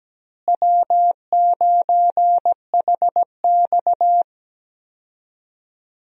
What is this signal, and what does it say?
Morse code, heard as a keyed tone, about 17 wpm, 700 Hz. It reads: W9HX
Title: USA Call Signs - 4 Characters - Repeated 2x - 17wpm